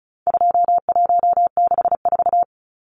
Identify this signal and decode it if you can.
Morse code, heard as a keyed tone, about 34 wpm, 700 Hz. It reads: 2164
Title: Numbers - 4-Digit 35wpm